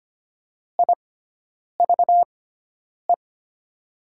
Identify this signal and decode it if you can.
Morse code, heard as a keyed tone, about 25 wpm, spaced as for 9 wpm, 700 Hz. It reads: IVE
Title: 6x Character Spacing - Top 300 Words - 25wpm